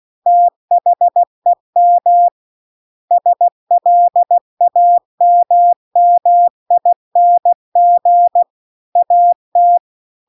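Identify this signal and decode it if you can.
Morse code, heard as a keyed tone, about 16 wpm, 700 Hz. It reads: THEM SLAMMING AT
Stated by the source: Morse practice text at 16 WPM.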